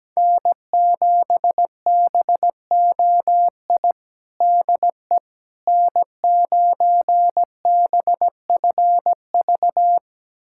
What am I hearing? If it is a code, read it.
Morse code, heard as a keyed tone, about 17 wpm, 700 Hz. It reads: N7BOI DE N9BFV
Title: USA Callsign DE Callsign - 5 Characters - 17wpm